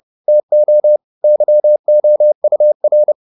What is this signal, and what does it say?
Morse code, heard as a keyed tone, about 30 wpm, 600 Hz. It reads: TO YOUR